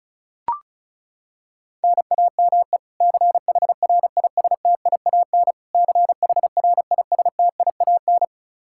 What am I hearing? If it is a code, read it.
Morse code, heard as a keyed tone, about 36 wpm, 700 Hz. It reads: NAME CHRISTIAN CHRISTIAN
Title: Practice QSO - Exchange 1 - Beginning - 35wpm